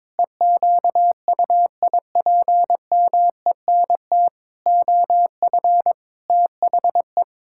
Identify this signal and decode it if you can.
Morse code, heard as a keyed tone, about 22 wpm, 700 Hz. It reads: EQUIPMENT OF THE
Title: Pangram Challenge v2 - 22wpm